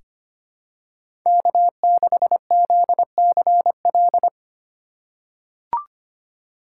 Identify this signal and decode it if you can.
Morse code, heard as a keyed tone, about 25 wpm, 700 Hz. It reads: K6ZCL
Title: K1USN SST QSO Practice - 25wpm